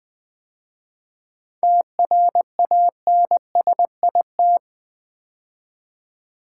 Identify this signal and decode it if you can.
Morse code, heard as a keyed tone, about 20 wpm, 700 Hz. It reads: TRANSIT
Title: 7 Letter Words - First 3 Letters Given - 20wpm